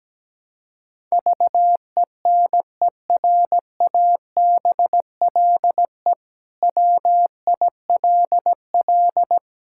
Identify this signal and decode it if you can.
Morse code, heard as a keyed tone, about 17 wpm, 700 Hz. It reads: VENERABLE WILL
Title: Pangram Challenge v2 - 17wpm